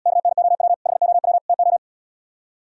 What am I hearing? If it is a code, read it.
Morse code, heard as a keyed tone, about 77 wpm, 700 Hz. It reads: KNOW HOW TO